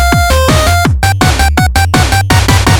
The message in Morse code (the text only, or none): none